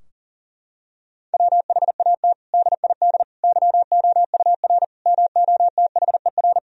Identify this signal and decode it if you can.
Morse code, heard as a keyed tone, about 40 wpm, 700 Hz. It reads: WHAT DID YOUR MOTHER
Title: Sentences from Top 1000 Words - Review Part 4 - 40wpm